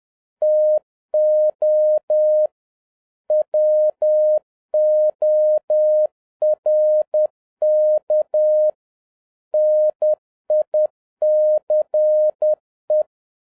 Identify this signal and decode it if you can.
Morse code, heard as a keyed tone, about 10 wpm, 610 Hz. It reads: TO WORK NICE
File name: 2011-07-mp3 / n56 BACK FROM A BIG ACCIDENT 10wpm